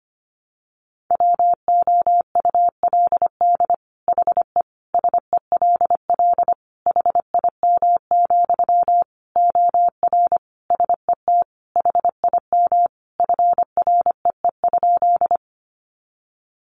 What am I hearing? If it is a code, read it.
Morse code, heard as a keyed tone, about 25 wpm, 700 Hz. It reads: WOULD HE SELL HIM, OR SET HIM FREE?
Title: Sentences from Top 1100 Words 25wpm